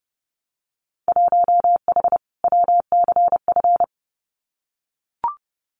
Morse code, heard as a keyed tone, about 30 wpm, 700 Hz. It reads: 1H WCF
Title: Winter Field Day Exchanges Encore 30wpm